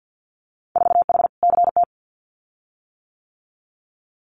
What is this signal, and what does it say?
Morse code, heard as a keyed tone, about 50 wpm, 700 Hz. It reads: 4H CT
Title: Winter Field Day Exchanges Encore 50wpm